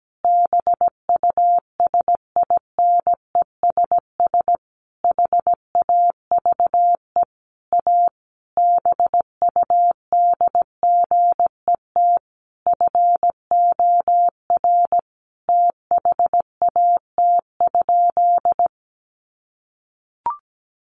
Morse code, heard as a keyed tone, about 17 wpm, 700 Hz. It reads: BUSINESS HAVE A BUDGET FOR THAT?
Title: Sentences from Top 1200 Words 17wpm